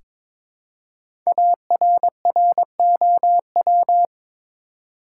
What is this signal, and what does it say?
Morse code, heard as a keyed tone, about 22 wpm, 700 Hz. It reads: ARROW
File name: 5 Letter Words - First 3 Letters Given - 22wpm